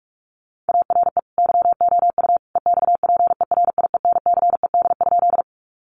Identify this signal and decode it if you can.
Morse code, heard as a keyed tone, about 44 wpm, 700 Hz. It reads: ARE YOU EXPERIENCED?
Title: Sentences from Top 1100 Words 45wpm